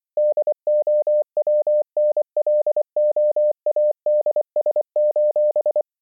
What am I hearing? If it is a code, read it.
Morse code, heard as a keyed tone, about 25 wpm, 600 Hz. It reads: DOWNLOADS: